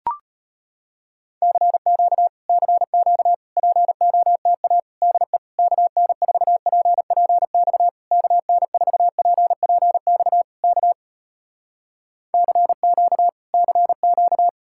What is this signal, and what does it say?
Morse code, heard as a keyed tone, about 36 wpm, 700 Hz. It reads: CQ CQ POTA DE KN4PPX KN4PPX K CQ CQ